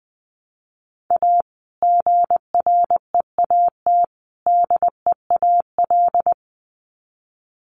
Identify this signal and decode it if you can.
Morse code, heard as a keyed tone, about 20 wpm, 700 Hz. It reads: A GREAT DEAL
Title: Sets of 3 Words 20wpm